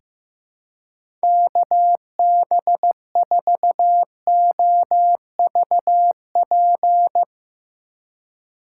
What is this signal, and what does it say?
Morse code, heard as a keyed tone, about 15 wpm, 700 Hz. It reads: KB4OVP
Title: USA Call Signs - 6 Characters - 3x Character Spacing - 15wpm